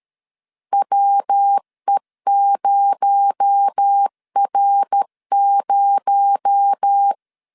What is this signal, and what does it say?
Morse code, heard as a keyed tone, about 13 wpm, 800 Hz. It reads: WE0R0
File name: LEZIONI / 6 PKX0 / 6 PKX0 12wpm 2231